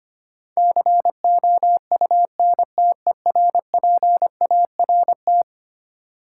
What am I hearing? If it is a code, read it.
Morse code, heard as a keyed tone, about 25 wpm, 700 Hz. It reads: COUNTERPART
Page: morse-code-ninja-practice-25wpm